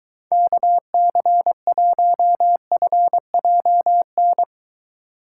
Morse code, heard as a keyed tone, about 23 wpm, 700 Hz. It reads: KC1FJN